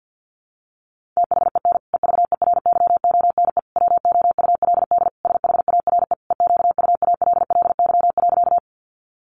Morse code, heard as a keyed tone, about 50 wpm, 700 Hz. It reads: THEN EVERYONE WOULD SHARE EQUALLY.
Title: Sentences from Top 1400 Words 50wpm